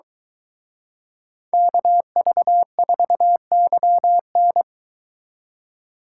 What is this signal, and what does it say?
Morse code, heard as a keyed tone, about 23 wpm, 700 Hz. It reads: KV4YN